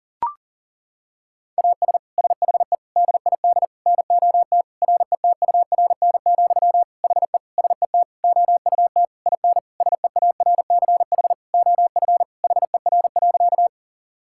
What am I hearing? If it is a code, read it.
Morse code, heard as a keyed tone, about 37 wpm, 700 Hz. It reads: AS SHE DID NOT RETURN, HE SET OUT IN SEARCH OF HER.
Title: Sentences from Top 1000 Words - Review Part 5 - 40wpm